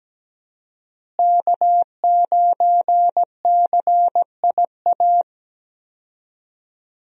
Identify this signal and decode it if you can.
Morse code, heard as a keyed tone, about 17 wpm, 700 Hz. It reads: K9CIA
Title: USA Call Signs - 5 Characters - Repeated 3x - 17wpm